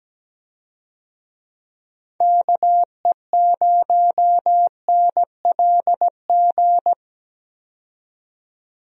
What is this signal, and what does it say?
Morse code, heard as a keyed tone, about 17 wpm, 700 Hz. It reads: KE0NLG
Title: USA Call Signs - 6 Characters - 17wpm